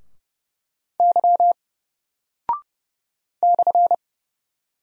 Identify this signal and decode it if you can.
Morse code, heard as a keyed tone, about 30 wpm, 700 Hz. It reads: Y /
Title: Single Character (Letter:Number:Punctuation:Prosign) Alt 30wpm